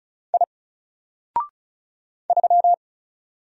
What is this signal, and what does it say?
Morse code, heard as a keyed tone, about 35 wpm, 700 Hz. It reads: I 3
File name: Single-Letter-Number-35wpm